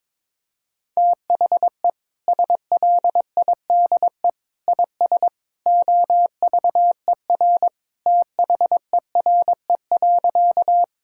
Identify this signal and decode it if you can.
Morse code, heard as a keyed tone, about 22 wpm, 700 Hz. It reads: THE SLIDE IS OVER THERE.
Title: Sentences from Top 1600 Words 22wpm